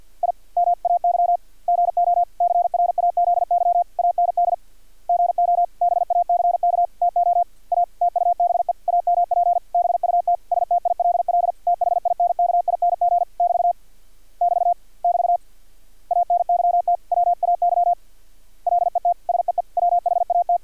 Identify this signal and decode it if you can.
Morse code, heard as a keyed tone, about 50 wpm, 700 Hz. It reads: I M AY GO CRAZY AND GO BACK TO A TUBE AMP BUT STILL THINKING <BT> <BT> <BT> ANYT WAY LET SEE WHAT
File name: archived / CRSnet